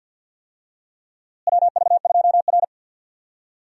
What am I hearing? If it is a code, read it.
Morse code, heard as a keyed tone, about 50 wpm, 700 Hz. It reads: WV2R
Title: USA Call Signs - 4 Characters - Repeated 2x - 50wpm